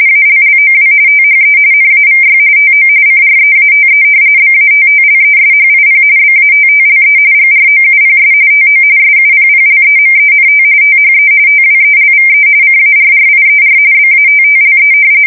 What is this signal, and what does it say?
RTTY, about 45.45 baud. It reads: ALLOW THE NAVY TO CREATE BATTLESHIP BATTLE GROUPS TO SUPPLEMENT THE TWELVE CARRIER BATTLE
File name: fsk170 / battleshipnj